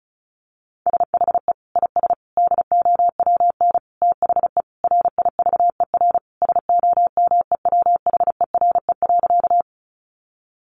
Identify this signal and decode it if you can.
Morse code, heard as a keyed tone, about 34 wpm, 700 Hz. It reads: SHE IS DOWN THE RIVER SOMEWHERE.
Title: Sentences from Top 1400 Words 35wpm